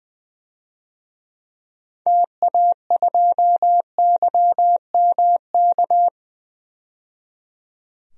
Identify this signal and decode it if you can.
Morse code, heard as a keyed tone, about 20 wpm, 700 Hz. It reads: TA2YMK
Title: DXCC-100 Call Signs - 6 Characters - 20wpm